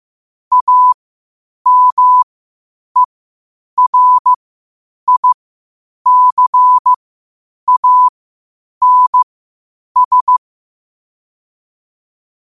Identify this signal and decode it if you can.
Morse code, heard as a keyed tone, about 15 wpm, spaced as for 8 wpm, 1000 Hz. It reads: AMERICANS
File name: qotdfiles / 8wpm